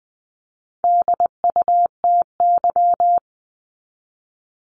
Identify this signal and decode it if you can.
Morse code, heard as a keyed tone, about 20 wpm, 700 Hz. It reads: DUTY
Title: Top 1251-1500 Words 20wpm